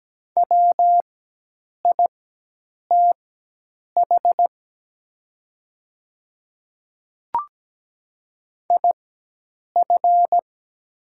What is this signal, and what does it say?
Morse code, heard as a keyed tone, about 17 wpm, spaced as for 8 wpm, 700 Hz. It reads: WITH IF
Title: 4x Character Spacing - Top 300 Words - 17wpm